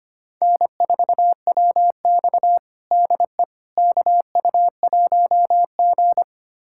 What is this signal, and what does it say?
Morse code, heard as a keyed tone, about 25 wpm, 700 Hz. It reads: N4WX DE KU1G
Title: USA Callsign DE Callsign - 4 Characters - 25wpm